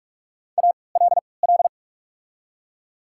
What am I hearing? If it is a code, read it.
Morse code, heard as a keyed tone, about 44 wpm, spaced as for 25 wpm, 690 Hz. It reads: ALL
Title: 3x Character Spacing - Top 100 Words - 45wpm